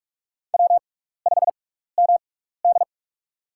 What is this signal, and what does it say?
Morse code, heard as a keyed tone, about 46 wpm, spaced as for 15 wpm, 700 Hz. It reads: W5MD
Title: USA Call Signs - 4 Characters - 6x Character Spacing - 45wpm